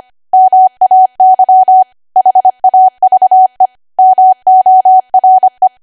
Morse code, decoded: MAY HAVE MORE